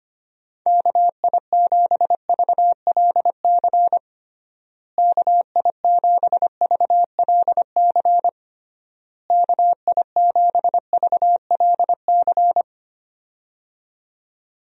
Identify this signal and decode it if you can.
Morse code, heard as a keyed tone, about 25 wpm, 700 Hz. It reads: KI7VLC KI7VLC KI7VLC
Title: USA Call Signs - 6 Characters - Repeated 3x - 25wpm